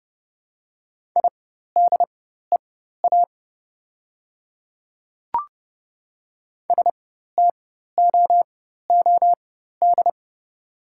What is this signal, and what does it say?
Morse code, heard as a keyed tone, about 31 wpm, spaced as for 14 wpm, 700 Hz. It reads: IDEA STOOD
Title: 4x Character Spacing - Top 500 Words - 30wpm